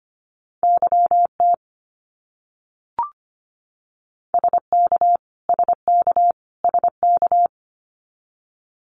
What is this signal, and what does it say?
Morse code, heard as a keyed tone, about 25 wpm, 700 Hz. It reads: YT SK SK SK
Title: Canadian Provinces and Territories - Repeated 3x - 25wpm